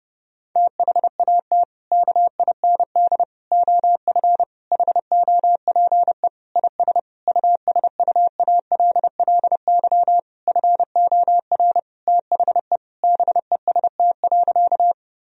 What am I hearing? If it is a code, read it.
Morse code, heard as a keyed tone, about 30 wpm, 700 Hz. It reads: THAT KIND OF HOPE IS USUALLY FOR THE BEST.
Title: Sentences from Top 1000 Words - Review Part 3 - 30wpm